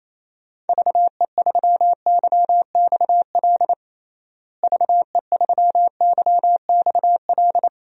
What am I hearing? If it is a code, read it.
Morse code, heard as a keyed tone, about 28 wpm, 700 Hz. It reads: VE3YXL VE3YXL
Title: DXCC-100 Call Signs - 6 Characters - Repeated 2x - 28wpm